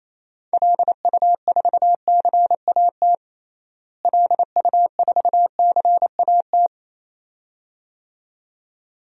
Morse code, heard as a keyed tone, about 27 wpm, 700 Hz. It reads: LU4CAT LU4CAT
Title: DXCC-100 Call Signs - 6 Characters - Repeated 2x - 28wpm